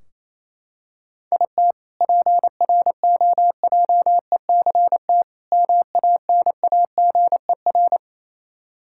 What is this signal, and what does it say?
Morse code, heard as a keyed tone, about 28 wpm, 700 Hz. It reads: IT PROJECT MANAGER
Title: K1USN SST QSO Practice 28wpm